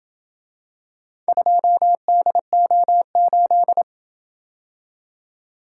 Morse code, heard as a keyed tone, about 26 wpm, 700 Hz. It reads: 2DO8